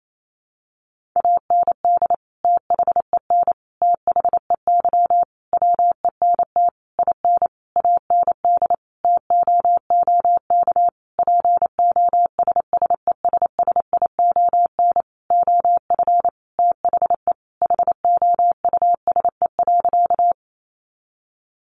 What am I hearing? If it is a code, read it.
Morse code, heard as a keyed tone, about 28 wpm, 700 Hz. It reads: AND THEN THEY WENT IN AND TOOK POSSESSION OF THE HOUSE.
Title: Sentences from Top 1400 Words 28wpm